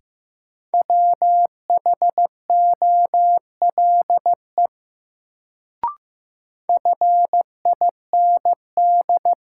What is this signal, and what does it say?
Morse code, heard as a keyed tone, about 15 wpm, 700 Hz. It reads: WHOLE FIND
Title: Sets of 2 Words - New Word Drill - 15wpm